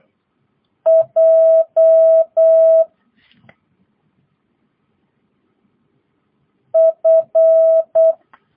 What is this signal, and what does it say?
Morse code, heard as a keyed tone, about 8 wpm, 650 Hz. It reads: J F
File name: jh8cbh / 2022cw